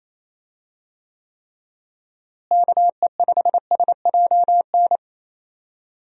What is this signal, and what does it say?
Morse code, heard as a keyed tone, about 28 wpm, 700 Hz. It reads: KE5SJN
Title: USA Call Signs - 6 Characters - Repeated 3x - 28wpm